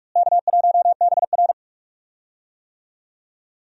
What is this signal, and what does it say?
Morse code, heard as a keyed tone, about 46 wpm, 700 Hz. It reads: K1BR
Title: USA Call Signs - 4 Characters - Repeated 3x - 45wpm